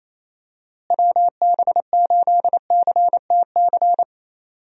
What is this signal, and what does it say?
Morse code, heard as a keyed tone, about 28 wpm, 700 Hz. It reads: WB8CTC
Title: USA Call Signs - 6 Characters - Repeated 3x - 28wpm